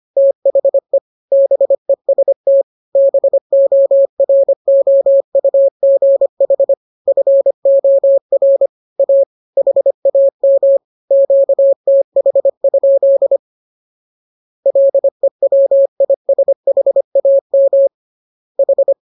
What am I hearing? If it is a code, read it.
Morse code, heard as a keyed tone, about 25 wpm, 550 Hz. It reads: THE BEST BOROUGH FOR A HAM QTH? LEWISHAM H
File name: jokes / joke 15 25 0000